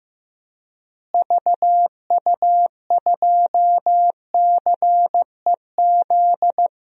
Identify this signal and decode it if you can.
Morse code, heard as a keyed tone, about 15 wpm, 700 Hz. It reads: VU2CEZ
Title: DXCC-100 Call Signs - 6 Characters - Repeated 2x - 15wpm